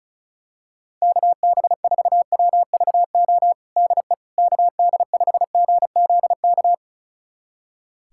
Morse code, heard as a keyed tone, about 34 wpm, 700 Hz. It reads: KB4WVO DE KD5GZK
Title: USA Callsign DE Callsign 35wpm